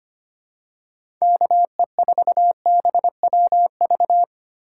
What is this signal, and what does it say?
Morse code, heard as a keyed tone, about 25 wpm, 700 Hz. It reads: KE4BWV
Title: USA Call Signs - 6 Characters - 4x Character Spacing - 25wpm